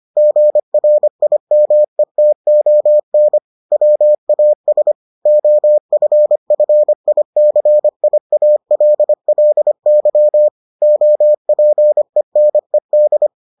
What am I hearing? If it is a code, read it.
Morse code, heard as a keyed tone, about 25 wpm, 600 Hz. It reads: GRIMETON WAS OFFICIALLY OPENED